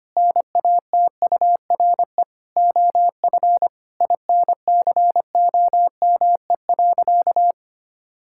Morse code, heard as a keyed tone, about 25 wpm, 700 Hz. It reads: NATURE OF INCOME.
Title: Sentences from Top 1000 Words - Review Part 5 - 25wpm